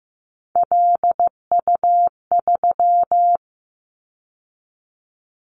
Morse code, heard as a keyed tone, about 15 wpm, 700 Hz. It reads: LU3
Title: Callsign Prefixes for Top 100 DXCC - 15wpm